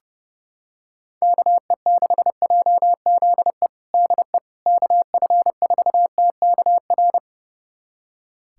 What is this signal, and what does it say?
Morse code, heard as a keyed tone, about 30 wpm, 700 Hz. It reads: KE6JZE DE KF4TKR
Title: USA Callsign DE Callsign - 6 Characters - 30wpm